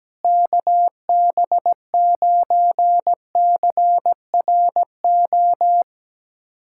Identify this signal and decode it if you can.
Morse code, heard as a keyed tone, about 17 wpm, 700 Hz. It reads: KB9CRO
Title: USA Callsign DE Callsign - 6 Characters - 17wpm